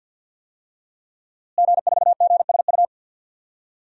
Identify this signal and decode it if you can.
Morse code, heard as a keyed tone, about 47 wpm, 700 Hz. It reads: G4GSU
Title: DXCC-100 Call Signs - 5 Characters - Repeated 3x - 50wpm